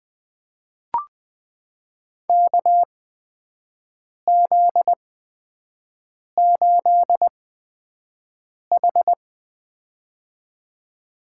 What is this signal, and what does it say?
Morse code, heard as a keyed tone, about 20 wpm, spaced as for 5 wpm, 700 Hz. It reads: KZ8H
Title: USA Call Signs - 8x Character Spacing - 20wpm